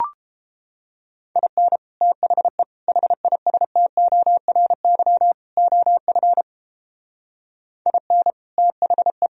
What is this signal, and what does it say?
Morse code, heard as a keyed tone, about 29 wpm, 700 Hz. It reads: IN THE HISTORY OF IN THE